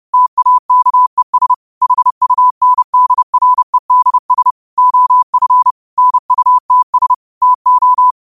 Morse code, heard as a keyed tone, about 30 wpm, 1000 Hz. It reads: TAKES HUNDREDS OF NUTS TO